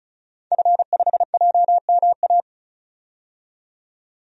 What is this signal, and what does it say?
Morse code, heard as a keyed tone, about 34 wpm, 700 Hz. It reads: F5JMA
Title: DXCC-100 Call Signs - 5 Characters - 35wpm